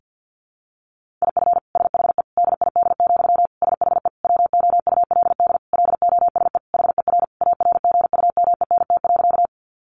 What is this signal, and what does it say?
Morse code, heard as a keyed tone, about 50 wpm, 700 Hz. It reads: IF SHE DID, SHE WOULD LOSE HER ARGUMENT.
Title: Sentences from Top 1400 Words 50wpm